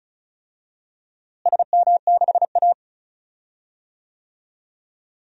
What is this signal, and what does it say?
Morse code, heard as a keyed tone, about 36 wpm, 700 Hz. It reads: SM6A
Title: DXCC-100 Call Signs - 4 Characters - Repeated 3x - 35wpm